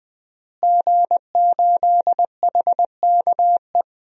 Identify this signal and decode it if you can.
Morse code, heard as a keyed tone, about 20 wpm, 700 Hz. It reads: G8HKE
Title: DXCC-100 Call Signs - 5 Characters - Repeated 3x - 20wpm